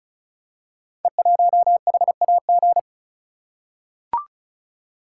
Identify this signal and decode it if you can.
Morse code, heard as a keyed tone, about 34 wpm, 690 Hz. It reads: E1HAG
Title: Code Groups - 5 Letter:Numbers - 35wpm